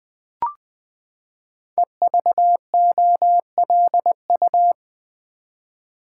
Morse code, heard as a keyed tone, about 20 wpm, 700 Hz. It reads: EVOLU
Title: Build A Word - 5 Letters At A Time - 12-Letter Words - 20wpm